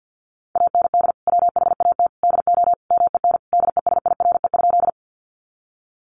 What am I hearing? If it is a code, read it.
Morse code, heard as a keyed tone, about 48 wpm, 700 Hz. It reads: AND WHAT DO MEN DESIRE?
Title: Sentences from Top 1100 Words 50wpm